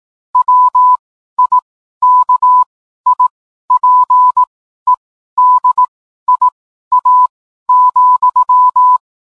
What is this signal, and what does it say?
Morse code, heard as a keyed tone, about 18 wpm, spaced as for 12 wpm, 1000 Hz. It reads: WIKIPEDIA,